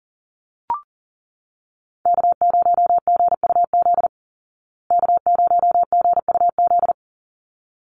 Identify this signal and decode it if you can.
Morse code, heard as a keyed tone, about 37 wpm, 700 Hz. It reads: K0GUZ K0GUZ
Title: Call Signs Repeated 3x - 40wpm